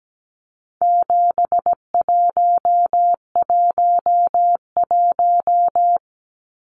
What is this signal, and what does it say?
Morse code, heard as a keyed tone, about 17 wpm, 700 Hz. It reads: 7111
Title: Numbers - 4-Digit 17wpm